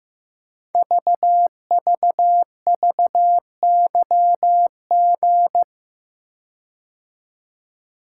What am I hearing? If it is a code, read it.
Morse code, heard as a keyed tone, about 15 wpm, 700 Hz. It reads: VVVYG
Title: Code Groups - 5 Letters - 15wpm